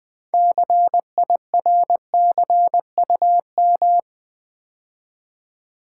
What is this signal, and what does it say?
Morse code, heard as a keyed tone, about 20 wpm, 700 Hz. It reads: CIRCUM
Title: Build A Word - 6 Letters At A Time - 12-Letter Words - 20wpm